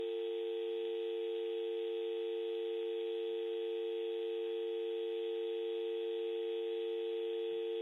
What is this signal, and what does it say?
Telephone call progress tone (UK), heard as a telephone network's tone, about 350, 450 Hz, with noise. Dial tone.